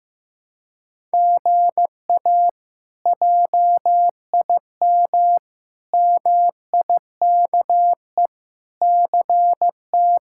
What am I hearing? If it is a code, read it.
Morse code, heard as a keyed tone, about 15 wpm, 700 Hz. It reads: GA JIM MIKE CT
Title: K1USN SST QSO Practice - 15wpm